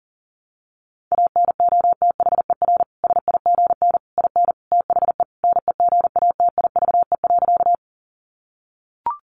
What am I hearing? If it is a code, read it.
Morse code, heard as a keyed tone, about 40 wpm, 700 Hz. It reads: ANOTHER SIGN IN THE NEGATIVE.
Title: Sentences from Top 1000 Words 40wpm